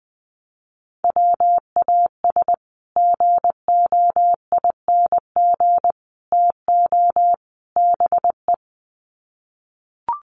Morse code, heard as a keyed tone, about 20 wpm, 700 Hz. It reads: WAS GOING TO BE
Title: Sets of 4 Words 20wpm